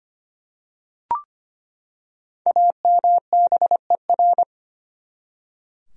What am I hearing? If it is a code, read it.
Morse code, heard as a keyed tone, about 25 wpm, 700 Hz. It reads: AMBER
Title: USA Common Names - 25wpm